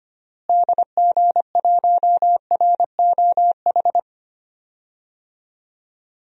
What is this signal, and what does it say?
Morse code, heard as a keyed tone, about 25 wpm, 700 Hz. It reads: DG1ROH
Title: DXCC-100 Call Signs - 6 Characters - Repeated 2x - 25wpm